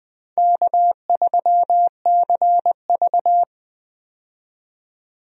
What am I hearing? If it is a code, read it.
Morse code, heard as a keyed tone, about 20 wpm, 700 Hz. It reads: K3CV
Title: USA Call Signs - 4 Characters - 20wpm